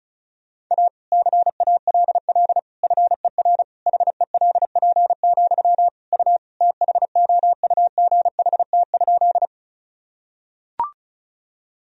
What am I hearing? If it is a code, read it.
Morse code, heard as a keyed tone, about 36 wpm, 700 Hz. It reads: A CALL FER HELP, U THOUGHT?
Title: Sentences from Top 300 Words and Abbreviations 35wpm